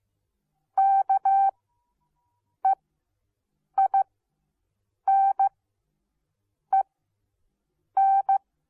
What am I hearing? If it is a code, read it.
Morse code, heard as a keyed tone, about 15 wpm, spaced as for 6 wpm, 780 Hz. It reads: KEINEN